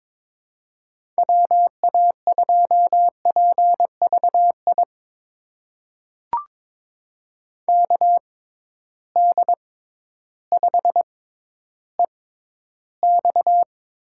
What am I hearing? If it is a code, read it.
Morse code, heard as a keyed tone, about 22 wpm, spaced as for 12 wpm, 700 Hz. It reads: WA2PVI KD5EX
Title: USA Call Signs - 6 Characters - 6x Character Spacing - 22wpm